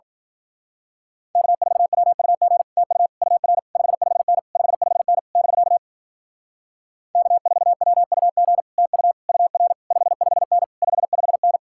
Morse code, heard as a keyed tone, about 47 wpm, 700 Hz. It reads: K4WUG TU UR 55N 55N <BK> K4WUG TU UR 55N 55N
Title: Speed-Racing Common QSO Phrases - 40wpm